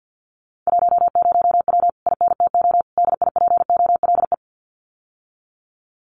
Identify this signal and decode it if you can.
Morse code, heard as a keyed tone, about 50 wpm, 700 Hz. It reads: 10W INTO DIPOLE
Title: Common QSO Phrases Encore 50wpm